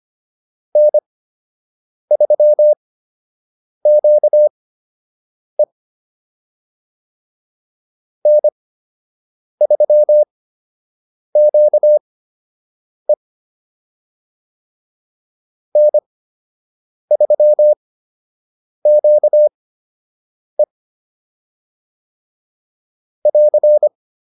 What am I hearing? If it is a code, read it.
Morse code, heard as a keyed tone, about 25 wpm, spaced as for 7 wpm, 600 Hz. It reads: N3QE N3QE N3QE <AR>